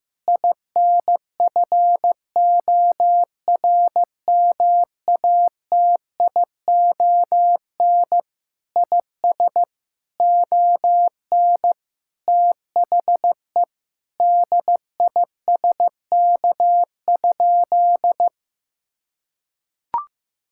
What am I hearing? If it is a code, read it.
Morse code, heard as a keyed tone, about 15 wpm, 700 Hz. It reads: INFORMATION IS ON THE DISK?
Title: Sentences from Top 1000 Words - Review Part 6 - 15wpm